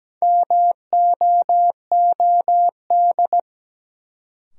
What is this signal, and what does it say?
Morse code, heard as a keyed tone, about 17 wpm, 700 Hz. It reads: MOOD
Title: Sentences from Top 1100 Words - New Word Drill - 17wpm